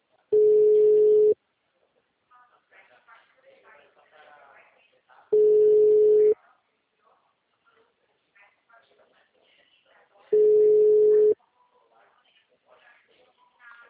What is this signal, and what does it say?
Ringback tone (Europe, 425 Hz), with speech.
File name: out- -2013